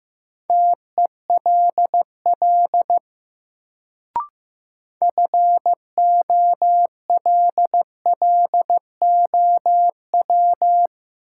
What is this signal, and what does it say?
Morse code, heard as a keyed tone, about 15 wpm, 700 Hz. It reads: TELL FOLLOW
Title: Sentences from Top 200 Words - New Word Drill - 15wpm